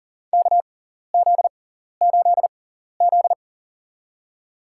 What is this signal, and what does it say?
Morse code, heard as a keyed tone, about 40 wpm, spaced as for 14 wpm, 700 Hz. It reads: KZ8Z
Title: USA Call Signs - 4 Characters - 6x Character Spacing - 40wpm